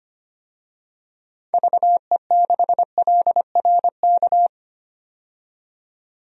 Morse code, heard as a keyed tone, about 25 wpm, 700 Hz. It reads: VE6LRK
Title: DXCC-100 Call Signs - 6 Characters - 25wpm